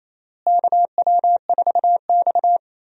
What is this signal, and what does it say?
Morse code, heard as a keyed tone, about 28 wpm, 700 Hz. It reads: KW4X
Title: USA Call Signs - 4 Characters - Repeated 2x - 28wpm